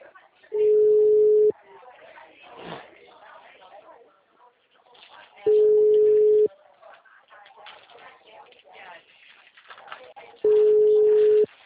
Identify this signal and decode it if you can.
Ringback tone (Europe, 425 Hz), with speech.
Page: TLMK - Rio de janeiro 2